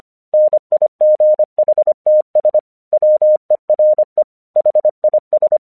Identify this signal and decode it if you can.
Morse code, heard as a keyed tone, about 24 wpm, 610 Hz. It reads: NIGHTS WERE HIS
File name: 1 English 25wpm / AQC13